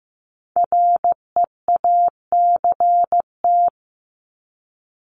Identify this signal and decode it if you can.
Morse code, heard as a keyed tone, about 15 wpm, 700 Hz. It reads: REACT
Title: Top 1251-1500 Words 15wpm